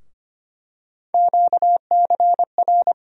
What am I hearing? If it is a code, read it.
Morse code, heard as a keyed tone, about 25 wpm, 700 Hz. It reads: QCR
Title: USA Call Sign Postfixes - 3 Letters 25wpm